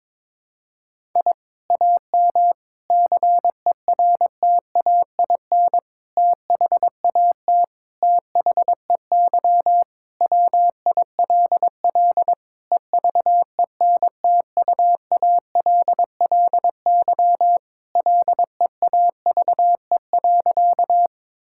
Morse code, heard as a keyed tone, about 22 wpm, 700 Hz. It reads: I AM CERTAIN THAT THEY WILL EVENTUALLY LEAVE.
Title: Sentences from Top 1000 Words - Review Part 6 - 22wpm